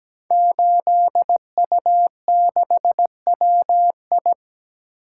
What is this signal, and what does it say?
Morse code, heard as a keyed tone, about 17 wpm, 700 Hz. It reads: 8U6WI
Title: Code Groups - 5 Letter:Numbers - 17wpm